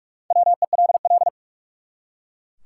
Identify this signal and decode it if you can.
Morse code, heard as a keyed tone, about 44 wpm, 700 Hz. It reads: WELL
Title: Top 180 QSO Elements 45wpm